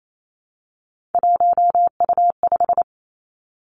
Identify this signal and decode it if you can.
Morse code, heard as a keyed tone, about 28 wpm, 700 Hz. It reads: 1U5
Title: Three Challenging Letter-Number Combinations 28wpm